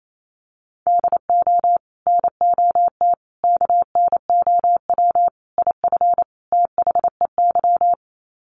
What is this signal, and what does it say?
Morse code, heard as a keyed tone, about 28 wpm, 700 Hz. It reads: DO NOT KNOW IF THEY
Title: Sets of 5 Words Encore 28wpm